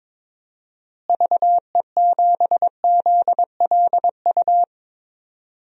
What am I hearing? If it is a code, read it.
Morse code, heard as a keyed tone, about 21 wpm, 700 Hz. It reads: VE7ZLU
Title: DXCC-100 Call Signs - 6 Characters - Repeated 3x - 22wpm